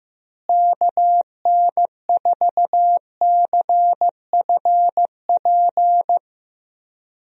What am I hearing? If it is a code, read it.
Morse code, heard as a keyed tone, about 15 wpm, 700 Hz. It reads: KN4CFP
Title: USA Call Signs - 6 Characters - Repeated 3x - 15wpm